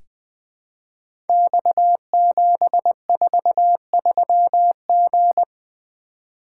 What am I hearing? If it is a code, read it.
Morse code, heard as a keyed tone, about 20 wpm, 700 Hz. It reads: X743G
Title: Code Groups - 5 Letter:Numbers - 20wpm